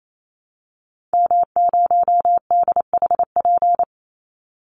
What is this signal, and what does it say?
Morse code, heard as a keyed tone, about 28 wpm, 700 Hz. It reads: M0DHP
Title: Call Signs - 28wpm